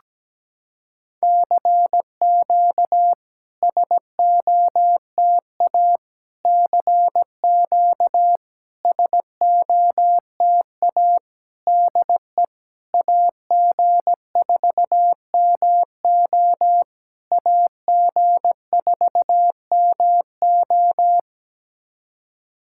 Morse code, heard as a keyed tone, about 17 wpm, 700 Hz. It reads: CQ SOTA CQ SOTA DE AG4MO AG4MO
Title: SOTA QSO Practice 17wpm